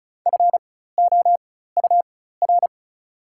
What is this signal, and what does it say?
Morse code, heard as a keyed tone, about 36 wpm, spaced as for 16 wpm, 690 Hz. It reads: FOUR
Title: 4x Character Spacing - Top 500 Words - 35wpm